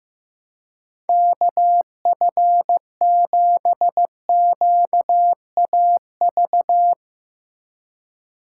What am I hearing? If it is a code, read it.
Morse code, heard as a keyed tone, about 16 wpm, 700 Hz. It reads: KF7QAV
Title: Speed Racing - USA Call Signs 15wpm